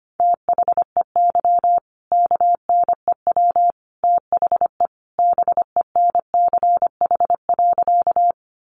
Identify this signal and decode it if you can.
Morse code, heard as a keyed tone, about 25 wpm, 700 Hz. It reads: THEY KNEW THE BENCH.
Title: Sentences from Top 1400 Words 25wpm